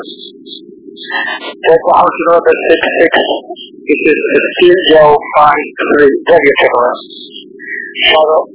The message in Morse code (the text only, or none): none